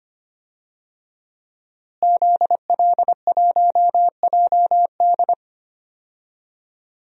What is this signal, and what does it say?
Morse code, heard as a keyed tone, about 25 wpm, 700 Hz. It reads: ZL1JD
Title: DXCC-100 Call Signs - 5 Characters - Repeated 3x - 25wpm